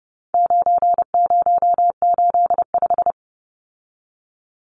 Morse code, heard as a keyed tone, about 30 wpm, 700 Hz. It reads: 9085
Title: Numbers - 4-Digit 30wpm